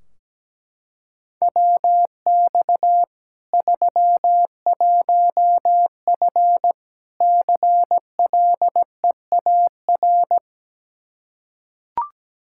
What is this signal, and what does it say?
Morse code, heard as a keyed tone, about 17 wpm, 700 Hz. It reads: WX 31F CLEAR
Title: Practice QSO - Exchange 2 - Info - 17wpm